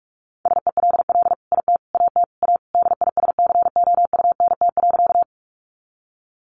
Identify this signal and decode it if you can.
Morse code, heard as a keyed tone, about 44 wpm, 700 Hz. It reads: SELL IT AT A DISCOUNT.
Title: Sentences from Top 1200 Words 45wpm